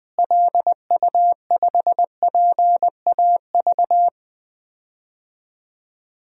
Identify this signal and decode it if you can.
Morse code, heard as a keyed tone, about 20 wpm, 700 Hz. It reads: LU5PAV
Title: DXCC-100 Call Signs - 6 Characters - Repeated 2x - 20wpm